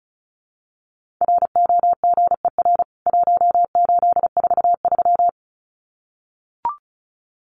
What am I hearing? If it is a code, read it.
Morse code, heard as a keyed tone, about 34 wpm, 700 Hz. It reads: ROGER 1843
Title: CWT Exchange 35wpm